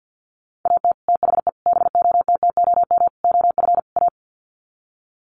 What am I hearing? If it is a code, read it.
Morse code, heard as a keyed tone, about 50 wpm, 700 Hz. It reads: AT THE BOTTOM OF A
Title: Sets of 5 Words Encore2 50wpm